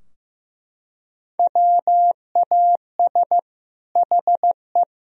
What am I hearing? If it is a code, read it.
Morse code, heard as a keyed tone, about 15 wpm, 700 Hz. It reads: WAS HE
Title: Sentences from Top 1000 Words - Review Part 4 - 15wpm